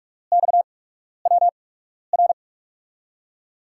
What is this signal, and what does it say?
Morse code, heard as a keyed tone, about 46 wpm, spaced as for 12 wpm, 700 Hz. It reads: XWR
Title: USA Call Signs - 5 Characters - 8x Character Spacing - 45wpm